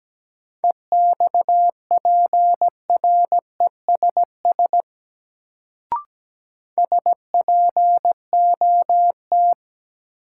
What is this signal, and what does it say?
Morse code, heard as a keyed tone, about 17 wpm, 700 Hz. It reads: EXPRESS SPOT
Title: Sentences from Top 1100 Words - New Word Drill - 17wpm